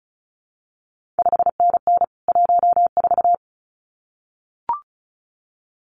Morse code, heard as a keyed tone, about 36 wpm, 700 Hz. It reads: 5NN 14
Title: CQ WW DX Contest Exchange - 35wpm